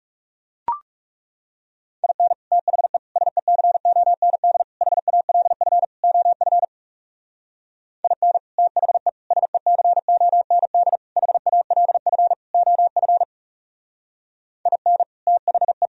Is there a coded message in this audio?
Yes: Morse code from 2.0 to 15.9 s, heard as a keyed tone, about 39 wpm, 700 Hz. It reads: IN THE SECOND HALF OF IN THE SECOND HALF OF IN THE